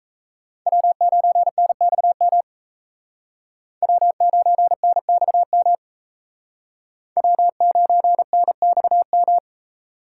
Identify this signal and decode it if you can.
Morse code, heard as a keyed tone, about 37 wpm, 700 Hz. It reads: W9NXM W9NXM W9NXM